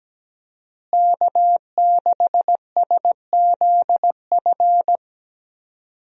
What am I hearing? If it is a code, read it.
Morse code, heard as a keyed tone, about 17 wpm, 700 Hz. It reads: K6SZF
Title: USA Call Signs - 5 Characters - Repeated 3x - 17wpm